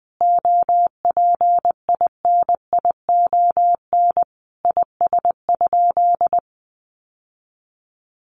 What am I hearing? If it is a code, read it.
Morse code, heard as a keyed tone, about 19 wpm, 700 Hz. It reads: OPINION IS?
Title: Sentences from Top 1200 Words 20wpm